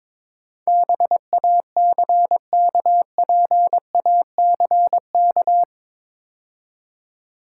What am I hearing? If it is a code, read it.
Morse code, heard as a keyed tone, about 24 wpm, 690 Hz. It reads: BACKPACK
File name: Easy Words - Speed-Racing with Hints - 22wpm